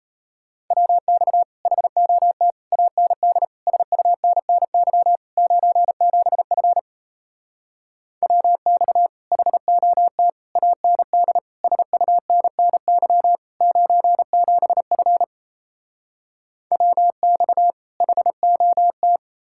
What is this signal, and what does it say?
Morse code, heard as a keyed tone, about 38 wpm, 700 Hz. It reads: WX HOT AND SUNNY 97F WX HOT AND SUNNY 97F WX HOT